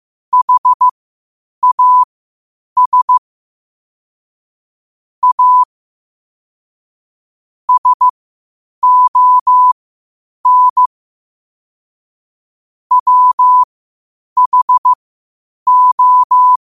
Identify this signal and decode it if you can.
Morse code, heard as a keyed tone, about 15 wpm, spaced as for 8 wpm, 1000 Hz. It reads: HAS A SON WHO
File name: qotdfiles / 8wpm